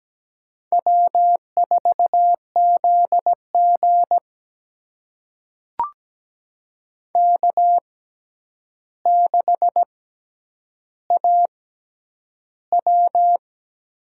Morse code, heard as a keyed tone, about 17 wpm, spaced as for 9 wpm, 700 Hz. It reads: W4ZG K6AW
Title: USA Call Signs - 4 Characters - 6x Character Spacing - 17wpm